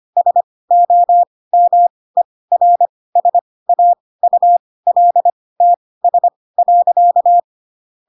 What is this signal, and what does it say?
Morse code, heard as a keyed tone, about 25 wpm, spaced as for 18 wpm, 700 Hz. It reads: SOMERSAULTS.
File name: Pinocchio / Pinocchio ch2 12 25 ik1hge